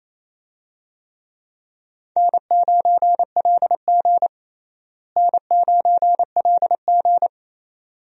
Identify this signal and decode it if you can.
Morse code, heard as a keyed tone, about 28 wpm, 700 Hz. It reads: N9LG N9LG
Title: USA Call Signs - 4 Characters - Repeated 2x - 28wpm